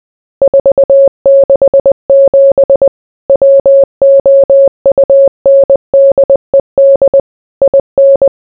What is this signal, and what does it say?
Morse code, heard as a keyed tone, about 20 wpm, 550 Hz. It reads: 467 WOUNDED IN